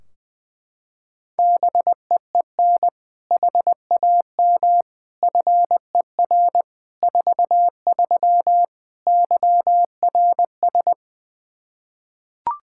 Morse code, heard as a keyed tone, about 21 wpm, 700 Hz. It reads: BEEN HAM FER 43 YRS
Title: Practice QSO - Exchange 3 - More Info - 20wpm